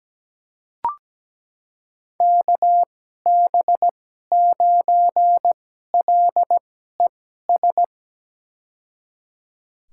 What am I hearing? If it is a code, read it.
Morse code, heard as a keyed tone, about 17 wpm, spaced as for 12 wpm, 700 Hz. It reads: KB9LES
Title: USA Call Signs - 6 Characters - 2x Character Spacing - 17wpm